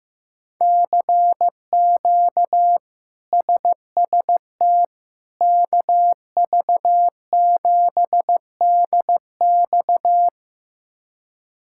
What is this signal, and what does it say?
Morse code, heard as a keyed tone, about 16 wpm, 700 Hz. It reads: CQ SST KV7DX
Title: K1USN SST QSO Practice - 15wpm